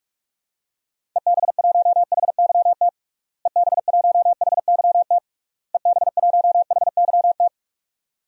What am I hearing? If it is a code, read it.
Morse code, heard as a keyed tone, about 41 wpm, 700 Hz. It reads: EB1HYT EB1HYT EB1HYT
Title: DXCC-100 Call Signs - 6 Characters - Repeated 3x - 45wpm